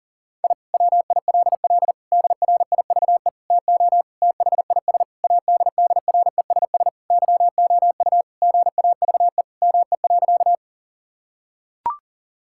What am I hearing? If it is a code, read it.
Morse code, heard as a keyed tone, about 40 wpm, 700 Hz. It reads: I WILL DRIVE TO THIS ADDRESS YOU GAVE ME.
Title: Sentences from Top 1000 Words - Review Part 6 - 40wpm